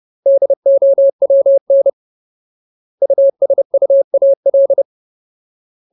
Morse code, heard as a keyed tone, about 30 wpm, 550 Hz. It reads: DOWN USUAL
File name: Ultimate Word List / 30WPM Ultimate